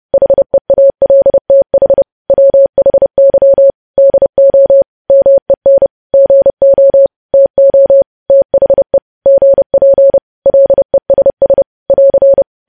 Morse code, heard as a keyed tone, about 30 wpm, 550 Hz. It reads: HEALTH WHY DO MEN GO TO THE GP LESS <AR>
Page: News Headlines in Morse Code at 30 WPM